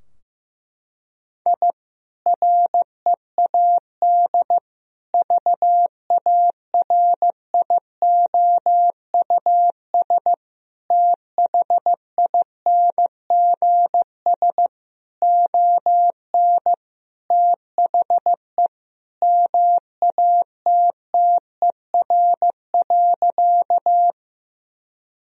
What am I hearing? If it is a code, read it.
Morse code, heard as a keyed tone, about 15 wpm, 700 Hz. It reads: I READ VARIOUS THINGS ON THE MATTER.
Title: Sentences from Top 1000 Words - Review Part 4 - 15wpm